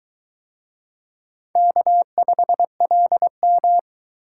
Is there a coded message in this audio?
Yes: Morse code from 1.6 to 3.8 s, heard as a keyed tone, about 20 wpm, 700 Hz. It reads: K5LM